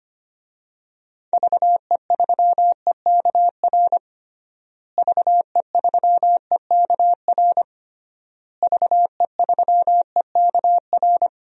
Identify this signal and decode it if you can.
Morse code, heard as a keyed tone, about 25 wpm, 700 Hz. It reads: VE3EKR VE3EKR VE3EKR
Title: DXCC-100 Call Signs - 6 Characters - Repeated 3x - 25wpm